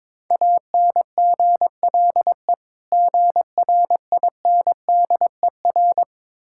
Morse code, heard as a keyed tone, about 22 wpm, 690 Hz. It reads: ANGLE GRINDER
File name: Difficult Words with Hints - 22wpm